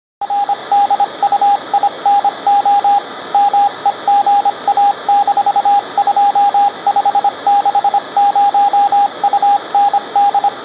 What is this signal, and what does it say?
Morse code, heard as a keyed tone, about 25 wpm, 810 Hz, with radio noise. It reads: RDUINO MEGA-2560 UND